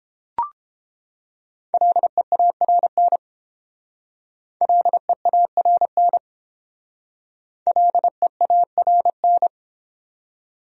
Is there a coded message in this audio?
Yes: Morse code from 1.7 to 9.5 s, heard as a keyed tone, about 32 wpm, 700 Hz. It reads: LEARN LEARN LEARN